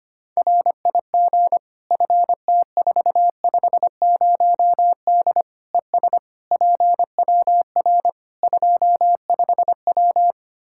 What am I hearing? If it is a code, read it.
Morse code, heard as a keyed tone, about 25 wpm, 700 Hz. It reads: RIG FT450D ES PWR 25W
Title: Practice QSO - Exchange 2 - Info - 25wpm